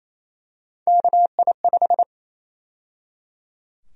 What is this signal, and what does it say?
Morse code, heard as a keyed tone, about 28 wpm, 700 Hz. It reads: KI5
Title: USA Call Sign Prefixes - 28wpm